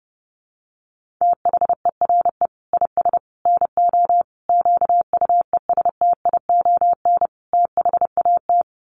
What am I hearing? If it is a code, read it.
Morse code, heard as a keyed tone, about 30 wpm, 700 Hz. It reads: THERE IS NO QUESTION THAT
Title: Sets of 5 Words Encore 30wpm